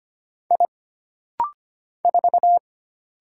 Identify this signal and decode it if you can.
Morse code, heard as a keyed tone, about 25 wpm, 700 Hz. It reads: I 4
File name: Single Character (Letter:Number:Punctuation:Prosign) - Rapid-Fire - 25wpm